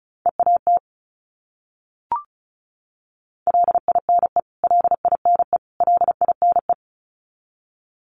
Morse code, heard as a keyed tone, about 34 wpm, 700 Hz. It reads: EAT LINE LINE LINE
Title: Top 500 Words Repeated 35wpm